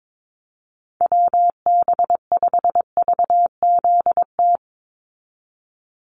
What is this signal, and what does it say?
Morse code, heard as a keyed tone, about 22 wpm, 700 Hz. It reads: WB5VZT
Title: USA Call Signs 22wpm